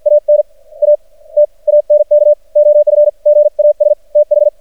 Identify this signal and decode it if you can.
Morse code, heard as a keyed tone, about 50 wpm, spaced as for 41 wpm, 590 Hz. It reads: AN A TANK OF GAS TH
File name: archived / CRSnet